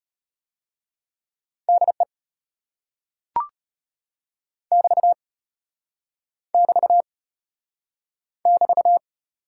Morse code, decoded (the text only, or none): DE <BT> <BT> <BT>